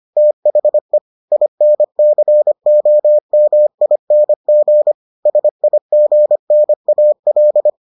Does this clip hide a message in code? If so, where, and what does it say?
Yes: Morse code from 0.2 to 7.7 s, heard as a keyed tone, about 25 wpm, 600 Hz. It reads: THE INCOMING SIGNAL